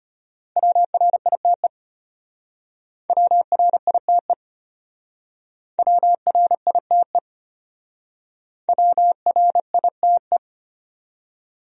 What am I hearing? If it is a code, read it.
Morse code, heard as a keyed tone, about 32 wpm, 700 Hz. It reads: WRITE WRITE WRITE WRITE